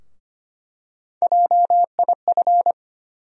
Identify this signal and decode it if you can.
Morse code, heard as a keyed tone, about 25 wpm, 700 Hz. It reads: JIF
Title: USA Call Sign Postfixes - 3 Letters 25wpm